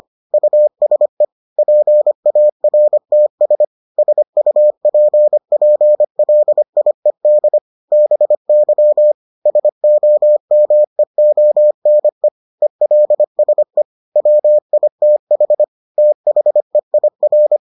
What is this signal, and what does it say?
Morse code, heard as a keyed tone, about 25 wpm, 600 Hz. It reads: USE PARTS SUPPLIED BY SOMEONE ELSE WITH THEIR